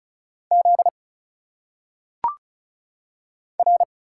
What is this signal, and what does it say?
Morse code, heard as a keyed tone, about 34 wpm, 700 Hz. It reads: Z R
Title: Single Letters - Phonetic Pronunciation 35wpm